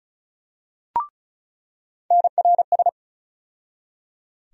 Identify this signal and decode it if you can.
Morse code, heard as a keyed tone, about 34 wpm, 700 Hz. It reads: NRS
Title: USA Call Sign Postfixes - 3 Letters 35wpm